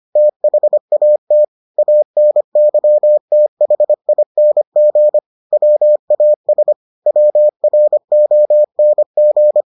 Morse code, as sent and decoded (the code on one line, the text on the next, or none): - .... .- - / .- -. -.-- - .... .. -. --. / .-- .- ... / .-- .-. --- -. --.
THAT ANYTHING WAS WRONG